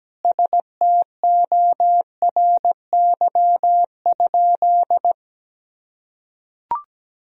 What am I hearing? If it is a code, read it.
Morse code, heard as a keyed tone, about 17 wpm, 700 Hz. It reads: STORY?
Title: Sentences from Top 400 Words and Abbreviations 17wpm